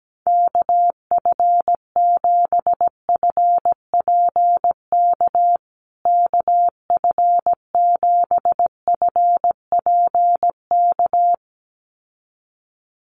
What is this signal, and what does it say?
Morse code, heard as a keyed tone, about 17 wpm, 700 Hz. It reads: KF7FPK KF7FPK
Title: USA Call Signs Repeated 2x 17wpm